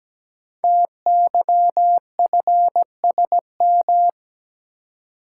Morse code, heard as a keyed tone, about 17 wpm, 700 Hz. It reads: TYFSM
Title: Code Groups - 5 Letters - 17wpm